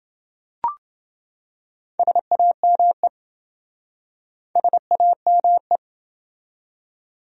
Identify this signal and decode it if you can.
Morse code, heard as a keyed tone, about 28 wpm, 700 Hz. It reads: SAME SAME